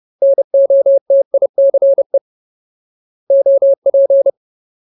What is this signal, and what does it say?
Morse code, heard as a keyed tone, about 30 wpm, 550 Hz. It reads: NOTICE OP
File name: Ultimate Word List / 30WPM Ultimate